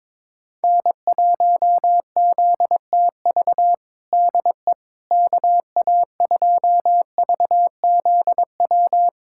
Morse code, heard as a keyed tone, about 22 wpm, 700 Hz. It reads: N1ZTV DE KA2VZW
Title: USA Callsign DE Callsign 22wpm